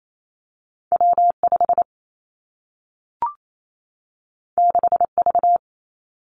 Morse code, heard as a keyed tone, about 28 wpm, 700 Hz. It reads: W 5 6 V
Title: Two Challenging Letter-Number Combinations 28wpm